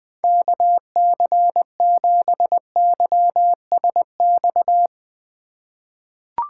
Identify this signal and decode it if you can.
Morse code, heard as a keyed tone, about 21 wpm, 700 Hz. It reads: KC7YSX
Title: USA Call Signs - 6 Characters - 2x Character Spacing - 20wpm